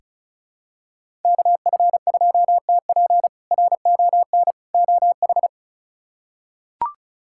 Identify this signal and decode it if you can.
Morse code, heard as a keyed tone, about 36 wpm, 700 Hz. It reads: KF2TP RON OH
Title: K1USN SST QSO Practice - 35wpm